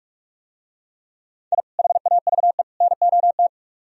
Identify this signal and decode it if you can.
Morse code, heard as a keyed tone, about 43 wpm, 700 Hz. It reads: I HAVE NOT